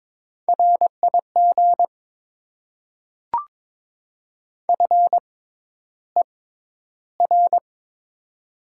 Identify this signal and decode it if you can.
Morse code, heard as a keyed tone, about 22 wpm, spaced as for 11 wpm, 700 Hz. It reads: RIG FER
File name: Top 20 QSO Elements - 6x Character Spacing - 22wpm